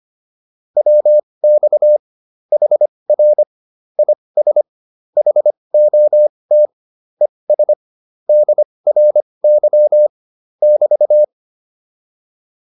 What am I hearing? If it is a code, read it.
Morse code, heard as a keyed tone, about 25 wpm, spaced as for 20 wpm, 600 Hz. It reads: WX HR IS HOT ES DRY <BT>
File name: QSO308a 20